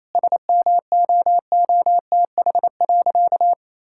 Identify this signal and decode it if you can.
Morse code, heard as a keyed tone, about 28 wpm, 700 Hz. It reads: SMOOTH.
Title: Sentences from Top 1600 Words 28wpm